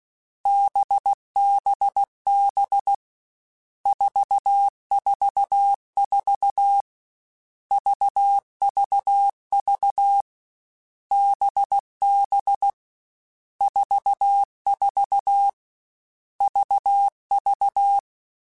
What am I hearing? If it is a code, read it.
Morse code, heard as a keyed tone, about 16 wpm, 800 Hz. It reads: BBB 444 VVV BB 44 VV
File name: Groups@13 / Groups@13 mp3s / VB46@13